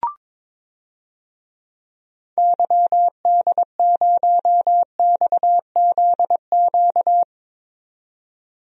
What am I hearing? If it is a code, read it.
Morse code, heard as a keyed tone, about 22 wpm, 700 Hz. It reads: YD0XZQ
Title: DXCC-100 Call Signs - 6 Characters - 22wpm